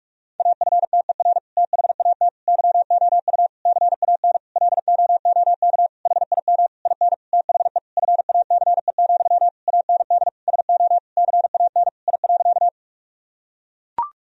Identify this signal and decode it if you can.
Morse code, heard as a keyed tone, about 42 wpm, 700 Hz. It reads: AFTER THAT YOU CAN LOOK HIM IN THE FACE, AND SO CAN I.
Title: Sentences from Top 1000 Words - Review Part 2 - 45wpm